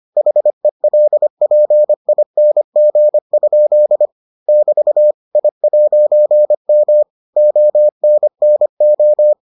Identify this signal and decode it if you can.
Morse code, heard as a keyed tone, about 25 wpm, 600 Hz. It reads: HELPING? <BT> I'M ONNO